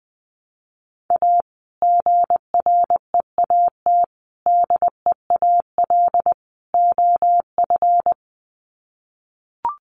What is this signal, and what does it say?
Morse code, heard as a keyed tone, about 20 wpm, 700 Hz. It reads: A GREAT DEAL OF
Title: Sets of 4 Words 20wpm